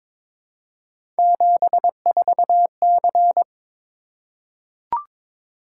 Morse code, heard as a keyed tone, about 21 wpm, 700 Hz. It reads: 74C
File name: practice / 22wpm / Three Letter-Number Combinations - 22wpm